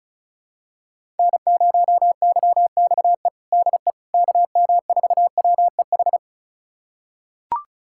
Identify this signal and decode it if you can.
Morse code, heard as a keyed tone, about 34 wpm, 700 Hz. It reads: N0YXE DE KM4WEH
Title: USA Callsign DE Callsign 35wpm